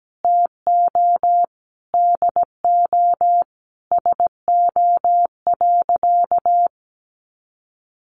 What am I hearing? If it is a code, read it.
Morse code, heard as a keyed tone, about 17 wpm, 700 Hz. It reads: TO DO SO.
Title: Sentences from Top 1500 Words 17wpm